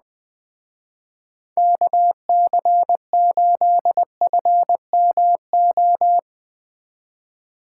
Morse code, heard as a keyed tone, about 21 wpm, 700 Hz. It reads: KC8FMO